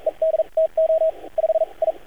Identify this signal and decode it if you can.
Morse code, heard as a keyed tone, about 42 wpm, spaced as for 37 wpm, 640 Hz, with radio noise. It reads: ED TO 5 I
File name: archived / CRSnet